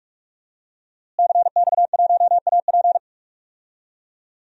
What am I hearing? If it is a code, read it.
Morse code, heard as a keyed tone, about 45 wpm, 700 Hz. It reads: CX1AP
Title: DXCC-100 Call Signs - 5 Characters - Repeated 2x - 45wpm